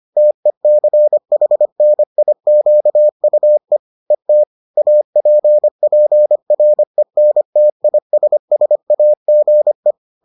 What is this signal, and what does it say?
Morse code, heard as a keyed tone, about 25 wpm, 600 Hz. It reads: TECHNIQUE ET APPRENTISSAGE
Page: bulletin UFT sous forme texte et en CW